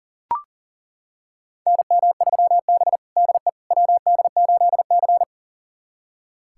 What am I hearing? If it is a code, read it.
Morse code, heard as a keyed tone, about 40 wpm, 700 Hz. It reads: NM3B DE WD8C
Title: USA Callsign DE Callsign - 4 Characters - 40wpm